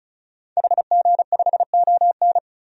Morse code, heard as a keyed tone, about 36 wpm, 700 Hz. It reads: HG5ON
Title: DXCC-100 Call Signs - 5 Characters - 35wpm